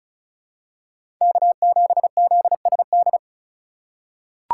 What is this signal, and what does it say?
Morse code, heard as a keyed tone, about 34 wpm, 700 Hz. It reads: K7ZSD
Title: USA Call Signs - 5 Characters - 4x Character Spacing - 35wpm